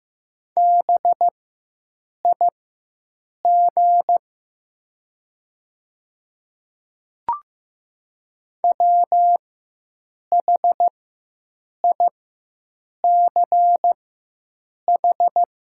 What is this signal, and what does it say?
Morse code, heard as a keyed tone, about 15 wpm, spaced as for 7 wpm, 700 Hz. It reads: BIG WHICH
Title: 4x Character Spacing - Top 200 Words - 15wpm